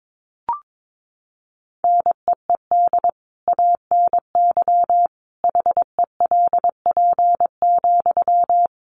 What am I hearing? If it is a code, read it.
Morse code, heard as a keyed tone, about 22 wpm, 700 Hz. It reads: NEED ANY HELP,
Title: Sentences from Top 1200 Words 22wpm